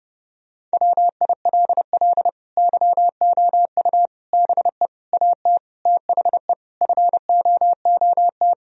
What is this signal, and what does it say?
Morse code, heard as a keyed tone, about 30 wpm, 700 Hz. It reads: WILL YOU BE AT THE FOOT
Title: Sentences from Top 1000 Words - Review Part 3 - 30wpm